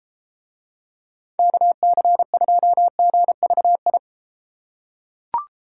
Morse code, heard as a keyed tone, about 30 wpm, 700 Hz. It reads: KC2GVI